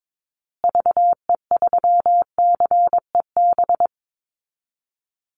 Morse code, heard as a keyed tone, about 22 wpm, 700 Hz. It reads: VE3CEB
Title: Call Signs Repeated 2x - 22wpm